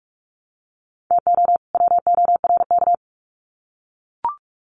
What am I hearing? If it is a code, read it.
Morse code, heard as a keyed tone, about 44 wpm, 700 Hz. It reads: TO WORK
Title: Sets of 2 Words 45wpm